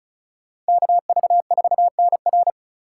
Morse code, heard as a keyed tone, about 45 wpm, 700 Hz. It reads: KV4NR